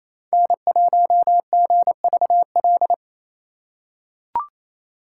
Morse code, heard as a keyed tone, about 27 wpm, 700 Hz. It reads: N1GVL
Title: USA Call Signs - 3x Character Spacing - 28wpm